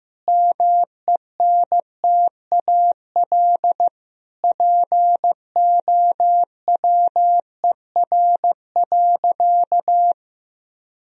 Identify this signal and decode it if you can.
Morse code, heard as a keyed tone, about 15 wpm, 700 Hz. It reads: MENTAL POWER.
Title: Sentences from Top 1000 Words - Review Part 4 - 15wpm